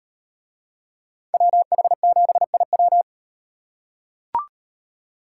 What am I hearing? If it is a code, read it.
Morse code, heard as a keyed tone, about 38 wpm, 700 Hz. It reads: WH7IW